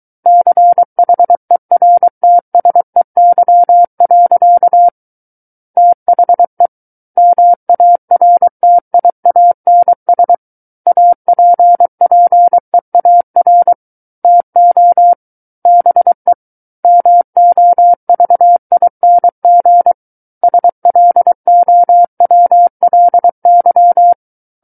Morse code, heard as a keyed tone, about 23 wpm, 700 Hz. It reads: CHERTSEY. THE MARTIANS APPEAR TO BE MOVING SLOWLY
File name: Book One / 14-Chapter 14 - 23 WPM